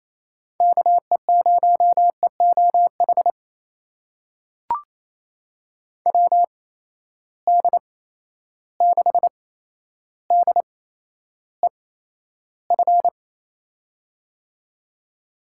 Morse code, heard as a keyed tone, about 28 wpm, spaced as for 12 wpm, 700 Hz. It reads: KE0EOH WD6DEF
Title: USA Call Signs - 6 Characters - 8x Character Spacing - 28wpm